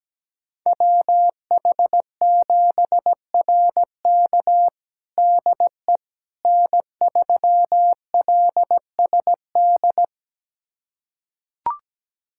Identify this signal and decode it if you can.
Morse code, heard as a keyed tone, about 17 wpm, 700 Hz. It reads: WH7RK DE N3LSD
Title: USA Callsign DE Callsign - 5 Characters - 17wpm